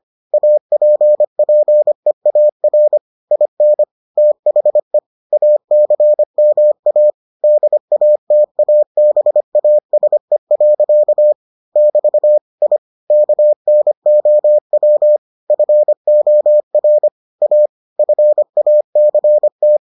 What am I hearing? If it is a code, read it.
Morse code, heard as a keyed tone, about 25 wpm, 600 Hz. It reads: APPEAR IN THE ACMA DATABASE. <BT> I KNOW FOR A FACT